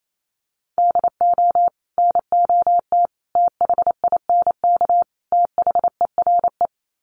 Morse code, heard as a keyed tone, about 28 wpm, 700 Hz. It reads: DO NOT THINK THERE
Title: Sets of 4 Words 28wpm